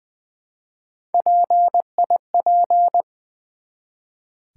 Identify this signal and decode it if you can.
Morse code, heard as a keyed tone, about 20 wpm, 700 Hz. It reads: PIP
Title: Build A Word - 1 Letter At A Time - 4-Letter Words - 20wpm